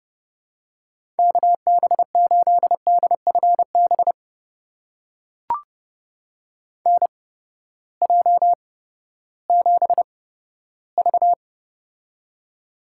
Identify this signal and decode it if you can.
Morse code, heard as a keyed tone, about 30 wpm, spaced as for 15 wpm, 700 Hz. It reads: KB8DFB NJ7V
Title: USA Call Signs - 8x Character Spacing - 30wpm